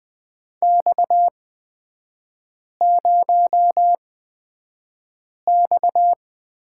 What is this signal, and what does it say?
Morse code, heard as a keyed tone, about 20 wpm, spaced as for 5 wpm, 700 Hz. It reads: X0X
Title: 250 Lesson 40e - BK - Three Characters 20wpm